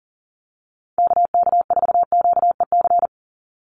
Morse code, heard as a keyed tone, about 40 wpm, 690 Hz. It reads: KK4QEC
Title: USA Call Signs 40wpm